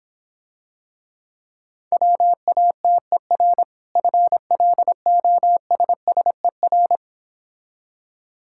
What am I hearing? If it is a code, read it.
Morse code, heard as a keyed tone, about 26 wpm, 700 Hz. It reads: WATER FLOSSER